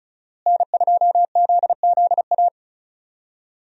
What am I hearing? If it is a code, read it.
Morse code, heard as a keyed tone, about 36 wpm, 700 Hz. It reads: N2ZZA
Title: USA Call Signs - 5 Characters - Repeated 2x - 35wpm